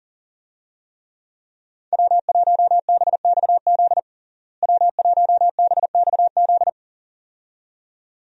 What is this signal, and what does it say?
Morse code, heard as a keyed tone, about 40 wpm, 700 Hz. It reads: W1BXZ W1BXZ
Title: USA Call Signs - 5 Characters - Repeated 2x - 40wpm